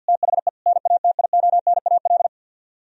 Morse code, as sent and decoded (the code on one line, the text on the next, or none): - .... / -. .- - .. --- -. .- .-..
THE NATIONAL